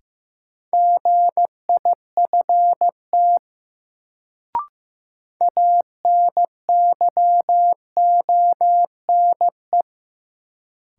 Morse code, heard as a keyed tone, about 15 wpm, 700 Hz. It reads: GIFT ANYONE
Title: Sentences from Top 900 Words - New Word Drill - 15wpm